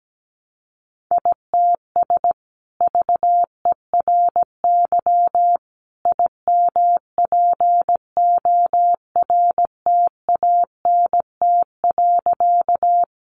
Morse code, heard as a keyed tone, about 17 wpm, 700 Hz. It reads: ITS VERY IMPORTANT.
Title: Sentences from Top 600 Words 17wpm